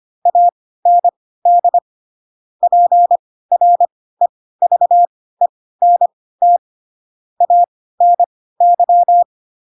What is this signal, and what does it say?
Morse code, heard as a keyed tone, about 25 wpm, spaced as for 16 wpm, 700 Hz. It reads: AND PREVENT ANY
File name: Pinocchio / Pinocchio ch3 10 25 ik1hge